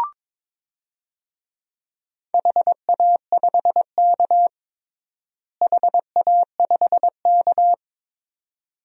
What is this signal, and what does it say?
Morse code, heard as a keyed tone, about 22 wpm, 700 Hz. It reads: HA5K HA5K
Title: DXCC-100 Call Signs - 4 Characters - Repeated 2x - 22wpm